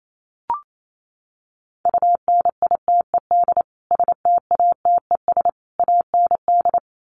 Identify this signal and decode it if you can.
Morse code, heard as a keyed tone, about 28 wpm, 700 Hz. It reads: UNITED STATES AND
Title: Sets of 3 Words 28wpm